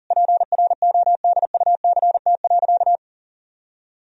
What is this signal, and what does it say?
Morse code, heard as a keyed tone, about 40 wpm, 700 Hz. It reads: PRODUCT.
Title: Sentences from Top 1000 Words - Review Part 2 - 40wpm